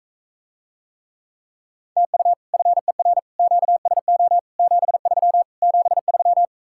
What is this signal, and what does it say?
Morse code, heard as a keyed tone, about 45 wpm, 700 Hz. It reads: TU FER QSO 73 73